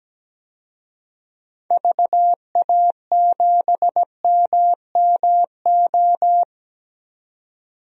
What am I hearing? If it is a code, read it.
Morse code, heard as a keyed tone, about 17 wpm, 700 Hz. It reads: VA7MMO
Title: DXCC-100 Call Signs - 6 Characters - 17wpm